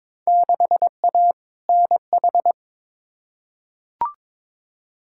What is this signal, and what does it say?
Morse code, heard as a keyed tone, about 22 wpm, 700 Hz. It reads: 6A NH
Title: Field Day Exchanges - 22wpm